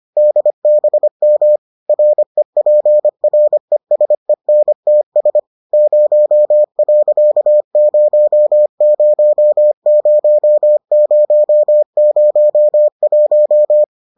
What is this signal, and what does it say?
Morse code, heard as a keyed tone, about 25 wpm, 600 Hz. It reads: DBM REPRESENTS 0.000001